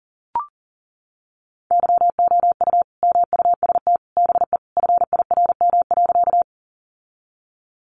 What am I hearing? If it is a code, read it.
Morse code, heard as a keyed tone, about 40 wpm, 700 Hz. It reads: YOU MUST BE FIRM.
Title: Sentences from Top 1400 Words 40wpm